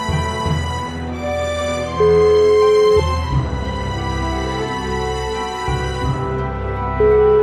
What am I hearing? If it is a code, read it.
Ringback tone (Europe, 425 Hz), with noise.